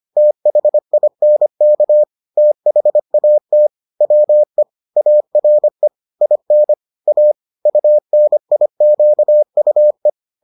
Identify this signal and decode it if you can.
Morse code, heard as a keyed tone, about 25 wpm, 600 Hz. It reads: THINK THAT WE ARE IN A UNIQUE